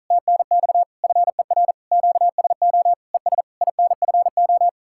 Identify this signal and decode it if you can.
Morse code, heard as a keyed tone, about 44 wpm, 700 Hz. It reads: TNX FER QSO ES INFO